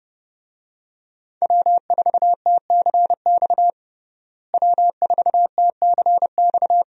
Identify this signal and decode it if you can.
Morse code, heard as a keyed tone, about 32 wpm, 700 Hz. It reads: W4TCX W4TCX
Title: USA Call Signs - 5 Characters - Repeated 2x - 30wpm